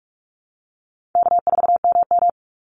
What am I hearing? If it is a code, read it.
Morse code, heard as a keyed tone, about 44 wpm, 700 Hz. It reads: K4MM
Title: Easy Call Signs Repeated 3x - 45wpm